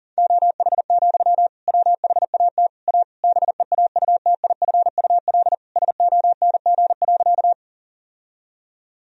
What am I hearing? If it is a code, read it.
Morse code, heard as a keyed tone, about 40 wpm, 700 Hz. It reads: OH, WHAT A BEAUTIFUL SONG.
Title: Sentences from Top 1000 Words - Review Part 2 - 40wpm